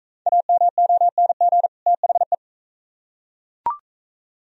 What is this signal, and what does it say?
Morse code, heard as a keyed tone, about 40 wpm, 700 Hz. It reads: AMONG THE